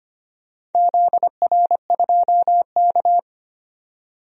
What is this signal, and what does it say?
Morse code, heard as a keyed tone, about 25 wpm, 700 Hz. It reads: ZR2K
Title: DXCC-100 Call Signs - 4 Characters - 25wpm